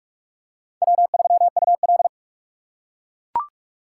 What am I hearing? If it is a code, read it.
Morse code, heard as a keyed tone, about 44 wpm, 700 Hz. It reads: W3UL